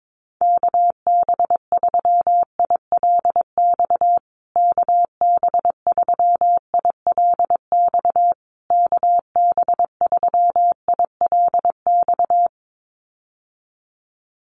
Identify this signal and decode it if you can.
Morse code, heard as a keyed tone, about 22 wpm, 700 Hz. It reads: KB3ILX KB3ILX KB3ILX
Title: USA Call Signs Repeated 3x 22wpm